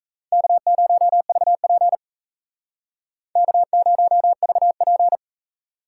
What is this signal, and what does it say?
Morse code, heard as a keyed tone, about 36 wpm, 700 Hz. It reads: K0VP K0VP